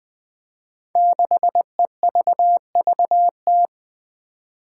Morse code, heard as a keyed tone, about 20 wpm, 700 Hz. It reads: 6EVVT
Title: Five Challenging Letter-Number Combinations 20wpm